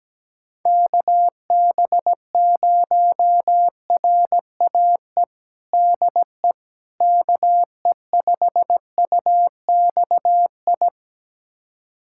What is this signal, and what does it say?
Morse code, heard as a keyed tone, about 17 wpm, 700 Hz. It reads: KB0RAE DE KE5UXI
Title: USA Callsign DE Callsign 17wpm